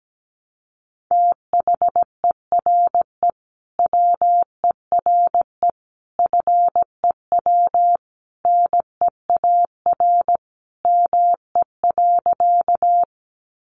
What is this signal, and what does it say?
Morse code, heard as a keyed tone, about 17 wpm, 700 Hz. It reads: THERE WERE FEW NEAR ME.
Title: Sentences from Top 300 Words 17wpm